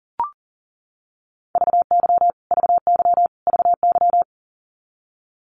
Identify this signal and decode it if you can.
Morse code, heard as a keyed tone, about 37 wpm, 700 Hz. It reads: VY VY VY
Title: Abbreviations Repeated 3x - 40wpm